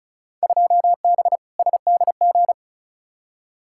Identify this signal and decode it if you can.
Morse code, heard as a keyed tone, about 36 wpm, 700 Hz. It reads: 2B SDG
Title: Field Day Exchanges Encore - 35wpm